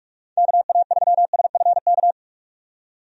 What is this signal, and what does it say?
Morse code, heard as a keyed tone, about 46 wpm, 700 Hz. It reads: KA3SFK
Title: USA Call Signs - 6 Characters - 45wpm